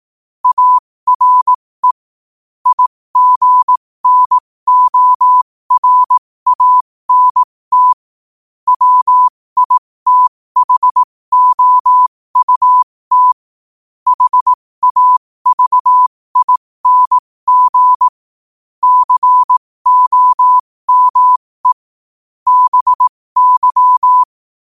Morse code, heard as a keyed tone, about 18 wpm, spaced as for 15 wpm, 1000 Hz. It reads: ARE IGNORANT WITHOUT HAVING COME BY
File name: qotdfiles / 15wpm